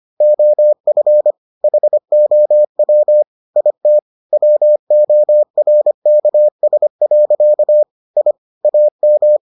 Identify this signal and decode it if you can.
Morse code, heard as a keyed tone, about 25 wpm, 600 Hz. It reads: OF HOW IT WORKS. I AM